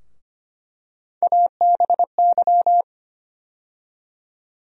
Morse code, heard as a keyed tone, about 25 wpm, 700 Hz. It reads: ABY
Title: USA Call Sign Postfixes - 3 Letters 25wpm